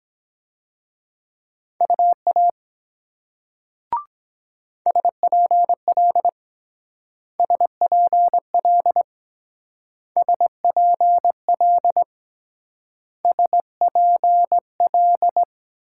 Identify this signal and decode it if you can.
Morse code, heard as a keyed tone, about 22 wpm, 700 Hz. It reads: UA SPL SPL SPL SPL